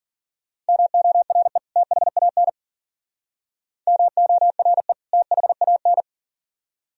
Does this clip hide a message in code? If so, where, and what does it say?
Yes: Morse code from 0.7 to 6.0 s, heard as a keyed tone, about 50 wpm, 700 Hz. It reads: MORE THAN MORE THAN